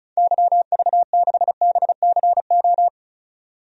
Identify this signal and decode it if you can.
Morse code, heard as a keyed tone, about 36 wpm, 700 Hz. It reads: YV6BCO
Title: DXCC-100 Call Signs - 6 Characters - 35wpm